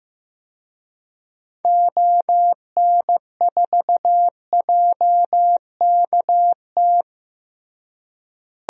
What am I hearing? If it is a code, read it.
Morse code, heard as a keyed tone, about 16 wpm, 700 Hz. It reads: ON4JKT
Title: DXCC-100 Call Signs - 6 Characters - Repeated 2x - 15wpm